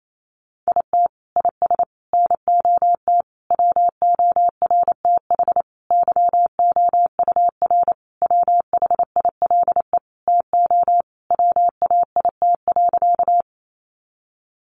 Morse code, heard as a keyed tone, about 28 wpm, 700 Hz. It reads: IT IS NOT WORTH YOUR WHILE TO WAIT.
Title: Sentences from Top 800 Words 28wpm